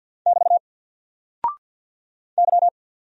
Morse code, heard as a keyed tone, about 49 wpm, 700 Hz. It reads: <BT> Y
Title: Single Character (Letter:Number:Punctuation:Prosign) 50wpm